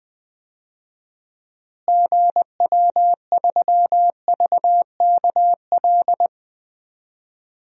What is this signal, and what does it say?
Morse code, heard as a keyed tone, about 20 wpm, 700 Hz. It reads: GW3VKL
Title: DXCC-100 Call Signs - 6 Characters - Repeated 3x - 20wpm